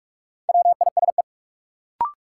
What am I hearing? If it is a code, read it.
Morse code, heard as a keyed tone, about 45 wpm, 700 Hz. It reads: WISE
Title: Sentences from Top 1600 Words - New Word Drill Alt - 45wpm